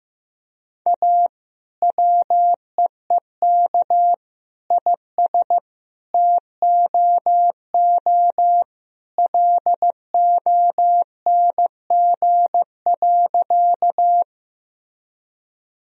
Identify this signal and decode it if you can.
Morse code, heard as a keyed tone, about 15 wpm, 700 Hz. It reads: A WEEK IS TOO LONG.
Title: Sentences from Top 1000 Words - Review Part 2 - 15wpm